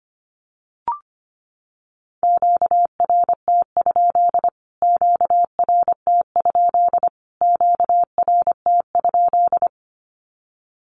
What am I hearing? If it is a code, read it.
Morse code, heard as a keyed tone, about 25 wpm, 700 Hz. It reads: QRT? QRT? QRT?
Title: Q-Codes Repeated 3x 25wpm